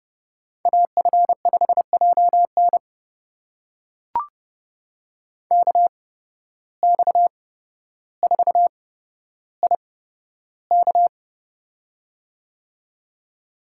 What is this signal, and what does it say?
Morse code, heard as a keyed tone, about 30 wpm, spaced as for 13 wpm, 700 Hz. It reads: AF5JN KX4IK
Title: USA Call Signs - 5 Characters - 8x Character Spacing - 30wpm